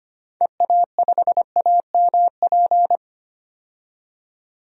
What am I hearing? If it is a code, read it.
Morse code, heard as a keyed tone, about 25 wpm, 700 Hz. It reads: EA5AMP
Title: DXCC-100 Call Signs - 6 Characters - 25wpm